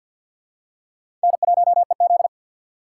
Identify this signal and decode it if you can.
Morse code, heard as a keyed tone, about 50 wpm, 700 Hz. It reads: N1EZ
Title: USA Call Signs - 4 Characters - 3x Character Spacing - 50wpm